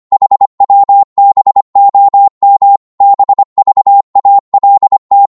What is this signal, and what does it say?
Morse code, heard as a keyed tone, about 25 wpm, 800 Hz. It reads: HWBOM BVALT
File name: Morse / ltr-25wpm